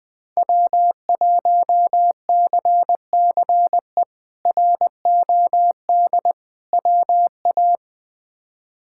Morse code, heard as a keyed tone, about 20 wpm, 700 Hz. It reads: W1CCE ROD WA
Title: K1USN SST QSO Practice - 20wpm